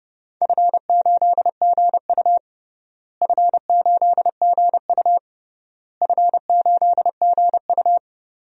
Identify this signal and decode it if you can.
Morse code, heard as a keyed tone, about 30 wpm, 700 Hz. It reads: F8GU F8GU F8GU
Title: DXCC-100 Call Signs - 4 Characters - Repeated 3x - 30wpm